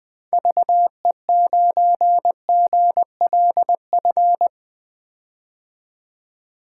Morse code, heard as a keyed tone, about 21 wpm, 700 Hz. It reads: VE9GLF
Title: K1USN SST QSO Practice - 20wpm